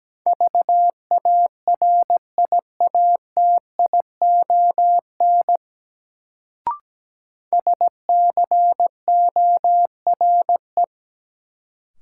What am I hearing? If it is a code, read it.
Morse code, heard as a keyed tone, about 17 wpm, 700 Hz. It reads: VARIATION SCORE
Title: Sentences from Top 1600 Words - New Word Drill - 17wpm